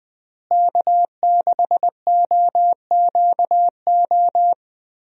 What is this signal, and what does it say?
Morse code, heard as a keyed tone, about 21 wpm, 700 Hz. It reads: K6OQO
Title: USA Call Signs - 5 Characters - Repeated 3x - 20wpm